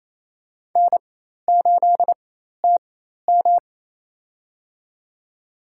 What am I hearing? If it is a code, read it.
Morse code, heard as a keyed tone, about 27 wpm, spaced as for 13 wpm, 700 Hz. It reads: N8TM
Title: USA Call Signs - 4 Characters - 4x Character Spacing - 28wpm